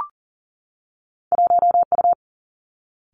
Morse code, heard as a keyed tone, about 38 wpm, 700 Hz. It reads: 1 U
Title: Two Challenging Letter-Number Combinations 40wpm